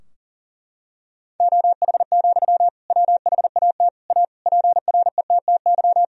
Morse code, heard as a keyed tone, about 37 wpm, 700 Hz. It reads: OH, WHAT A PRETTY
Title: Sentences from Top 1000 Words - Review Part 2 - 40wpm